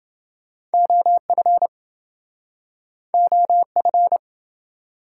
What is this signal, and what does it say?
Morse code, heard as a keyed tone, about 27 wpm, 700 Hz. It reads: OF OF